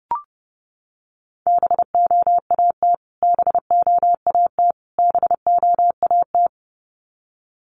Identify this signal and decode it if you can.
Morse code, heard as a keyed tone, about 30 wpm, 700 Hz. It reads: BOAT BOAT BOAT
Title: Top 500 Words Repeated 30wpm